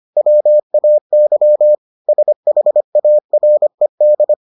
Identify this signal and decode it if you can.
Morse code, heard as a keyed tone, about 25 wpm, 600 Hz. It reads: WAY SHARED